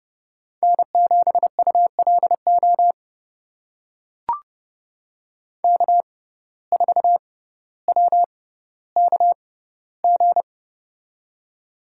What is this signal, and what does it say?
Morse code, heard as a keyed tone, about 31 wpm, spaced as for 16 wpm, 700 Hz. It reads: N7ULO K4WKG
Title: USA Call Signs - 5 Characters - 6x Character Spacing - 30wpm